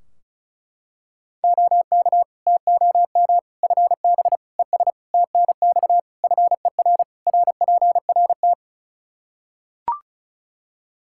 Morse code, heard as a keyed tone, about 36 wpm, 700 Hz. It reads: OK TOM FB ES TNX FER RPRT
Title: Practice QSO - Exchange 2 - Info - 35wpm